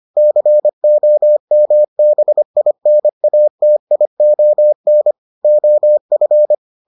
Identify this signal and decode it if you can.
Morse code, heard as a keyed tone, about 25 wpm, 600 Hz. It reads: COMBINATION OF